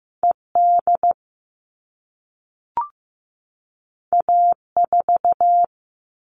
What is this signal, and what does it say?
Morse code, heard as a keyed tone, about 15 wpm, 700 Hz. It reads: ED A4
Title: All Cut Numbers - 2-Digit 15wpm